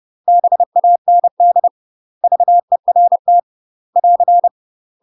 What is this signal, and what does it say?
Morse code, heard as a keyed tone, about 30 wpm, 700 Hz. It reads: BAND VERT <AR>
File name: UK MASTER QSOs TRAINING / 30WPM / UK MASTER QSOs 0042